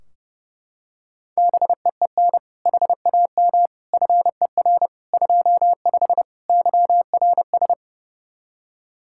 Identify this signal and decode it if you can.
Morse code, heard as a keyed tone, about 32 wpm, 700 Hz. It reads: BEEN HAM FER 25 YRS
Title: K1USN SST QSO Practice 30wpm